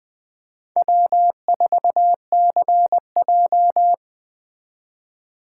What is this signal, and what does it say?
Morse code, heard as a keyed tone, about 21 wpm, 700 Hz. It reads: W4CJ
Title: USA Call Signs - 3x Character Spacing - 20wpm